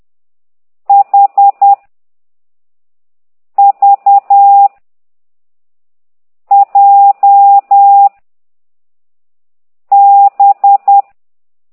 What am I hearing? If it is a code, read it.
Morse code, heard as a keyed tone, about 10 wpm, 800 Hz. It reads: H V J B
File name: ct2idw / exerc9